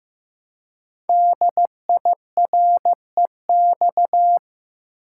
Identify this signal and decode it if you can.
Morse code, heard as a keyed tone, about 15 wpm, 700 Hz. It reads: DIREX
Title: Code Groups - 5 Letters - 15wpm